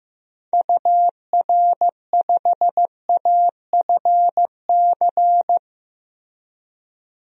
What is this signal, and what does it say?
Morse code, heard as a keyed tone, about 15 wpm, 700 Hz. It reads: UR5AFC
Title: DXCC-100 Call Signs - 6 Characters - Repeated 2x - 15wpm